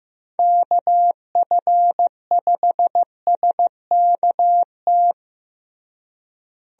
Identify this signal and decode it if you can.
Morse code, heard as a keyed tone, about 15 wpm, 700 Hz. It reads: KF5SKT
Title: USA Call Signs - 6 Characters - Repeated 2x - 15wpm